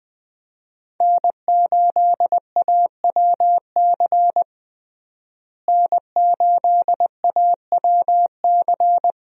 Morse code, heard as a keyed tone, about 20 wpm, 700 Hz. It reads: N8AWC N8AWC
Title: USA Call Signs - 5 Characters - Repeated 2x - 20wpm